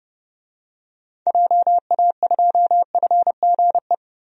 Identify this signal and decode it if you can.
Morse code, heard as a keyed tone, about 30 wpm, 700 Hz. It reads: JA2FGE
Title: DXCC-100 Call Signs - 6 Characters - Repeated 2x - 30wpm